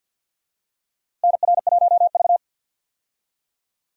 Morse code, heard as a keyed tone, about 49 wpm, 700 Hz. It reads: NR1V
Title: USA Call Signs - 4 Characters - Repeated 2x - 50wpm